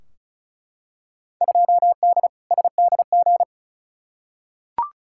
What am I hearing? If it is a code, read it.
Morse code, heard as a keyed tone, about 35 wpm, 700 Hz. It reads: 2D SDG
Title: Field Day Exchanges Encore - 35wpm